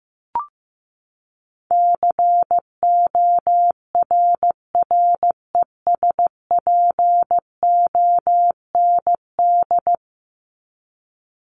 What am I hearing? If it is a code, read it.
Morse code, heard as a keyed tone, about 15 wpm, 700 Hz. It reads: CORRESPOND
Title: ING Suffix - Top 100 Words with Base Words 15wpm